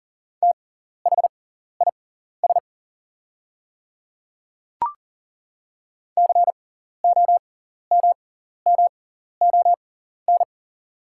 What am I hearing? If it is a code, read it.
Morse code, heard as a keyed tone, about 40 wpm, spaced as for 14 wpm, 690 Hz. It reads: THIS COMMON
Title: 6x Character Spacing - Top 500 Words - 40wpm